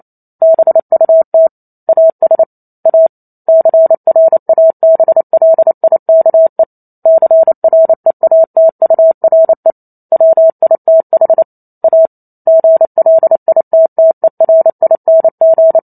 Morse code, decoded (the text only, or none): BUT AS A CRABLIKE CREATURE WITH A GLITTERING